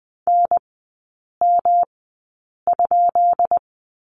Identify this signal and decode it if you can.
Morse code, heard as a keyed tone, about 20 wpm, spaced as for 8 wpm, 690 Hz. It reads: NM?
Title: 133 Lesson 22e - Question Mark - Three Characters 20wpm